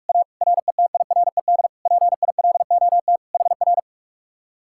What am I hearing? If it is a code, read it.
Morse code, heard as a keyed tone, about 48 wpm, 700 Hz. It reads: A RETIRED PILOT HR